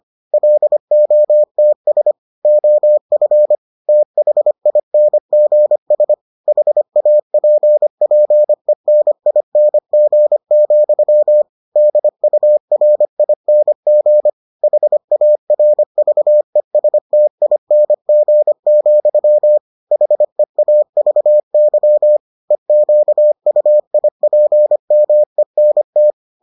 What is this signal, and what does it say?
Morse code, heard as a keyed tone, about 25 wpm, 600 Hz. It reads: LOTS OF THINGS HAPPENING, DURING HARVESTING, HEAVY EQUIPMENT